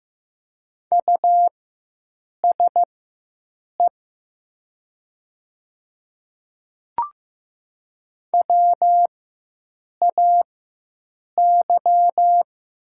Morse code, heard as a keyed tone, about 15 wpm, spaced as for 7 wpm, 700 Hz. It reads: USE WAY
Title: 4x Character Spacing - Top 200 Words - 15wpm